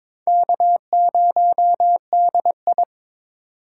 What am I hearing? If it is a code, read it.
Morse code, heard as a keyed tone, about 22 wpm, 700 Hz. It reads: K0DI
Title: USA Call Signs - 4x Character Spacing - 22wpm